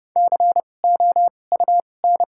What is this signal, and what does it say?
Morse code, heard as a keyed tone, about 30 wpm, spaced as for 22 wpm, 700 Hz. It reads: COUN
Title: 2x Character Spacing - Top 300 Words - 30wpm